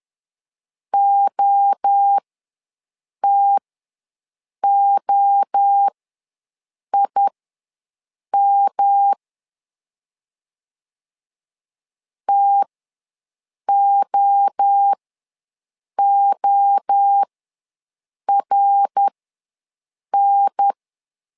Morse code, heard as a keyed tone, about 10 wpm, spaced as for 6 wpm, 800 Hz. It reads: OTOIM TOORN
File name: LEZIONI / 3 MWRO / 3 MWRO ws 10wpm 2310